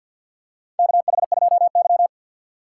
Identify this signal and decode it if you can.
Morse code, heard as a keyed tone, about 52 wpm, 700 Hz. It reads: KH2Y
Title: USA Call Signs - 3x Character Spacing - 50wpm